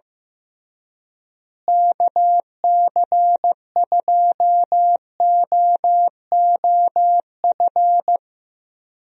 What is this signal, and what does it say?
Morse code, heard as a keyed tone, about 16 wpm, 700 Hz. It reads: KC2OOF
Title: USA Call Signs - 6 Characters - Repeated 3x - 15wpm